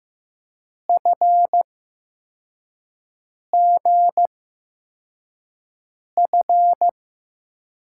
Morse code, heard as a keyed tone, about 15 wpm, spaced as for 4 wpm, 700 Hz. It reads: FGF
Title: USA Call Signs - 6 Characters - 8x Character Spacing - 15wpm